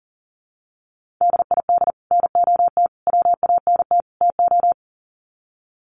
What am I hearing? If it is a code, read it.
Morse code, heard as a keyed tone, about 40 wpm, 700 Hz. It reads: DID NOT WANT TO
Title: Sets of 4 Words 40wpm